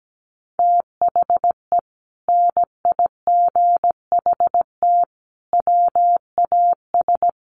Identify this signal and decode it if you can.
Morse code, heard as a keyed tone, about 17 wpm, 700 Hz. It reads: THE NIGHT WAS
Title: Sentences from Top 1200 Words 17wpm